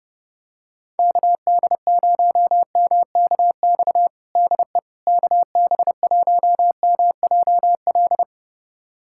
Morse code, heard as a keyed tone, about 30 wpm, 700 Hz. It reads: KD0MKX DE KB1MJL
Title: USA Callsign DE Callsign - 6 Characters - 30wpm